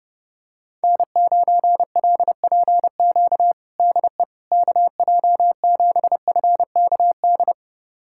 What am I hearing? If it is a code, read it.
Morse code, heard as a keyed tone, about 30 wpm, 700 Hz. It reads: N9LPQ DE KJ7FKD
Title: USA Callsign DE Callsign 30wpm